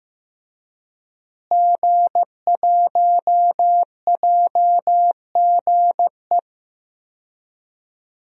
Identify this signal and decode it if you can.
Morse code, heard as a keyed tone, about 15 wpm, 700 Hz. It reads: G1JGE
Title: DXCC-100 Call Signs - 5 Characters - Repeated 3x - 15wpm